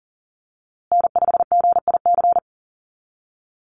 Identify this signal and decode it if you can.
Morse code, heard as a keyed tone, about 38 wpm, 700 Hz. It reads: N5GIC
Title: USA Call Signs Repeated 2x 40wpm